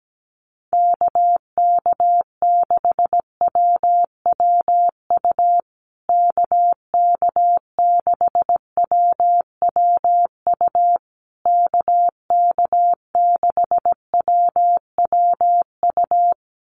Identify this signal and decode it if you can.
Morse code, heard as a keyed tone, about 17 wpm, 700 Hz. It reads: KK6WWU KK6WWU KK6WWU
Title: USA Call Signs Repeated 3x 17wpm